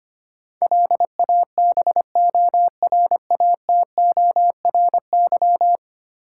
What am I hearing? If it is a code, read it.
Morse code, heard as a keyed tone, about 25 wpm, 700 Hz. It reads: LABORATORY
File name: 10 Letter Words - First 3 Letters Given - 25wpm